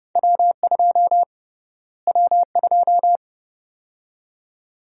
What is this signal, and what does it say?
Morse code, heard as a keyed tone, about 30 wpm, 700 Hz. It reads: W2 W2
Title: USA Call Sign Prefixes - Repeated 2x - 30wpm